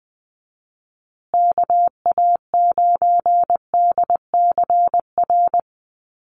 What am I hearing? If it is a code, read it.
Morse code, heard as a keyed tone, about 20 wpm, 700 Hz. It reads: KA9DCR
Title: USA Call Signs Repeated 2x 20wpm